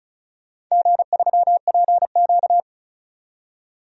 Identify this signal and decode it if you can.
Morse code, heard as a keyed tone, about 34 wpm, 700 Hz. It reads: G3PQ
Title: DXCC-100 Call Signs - 4 Characters - 35wpm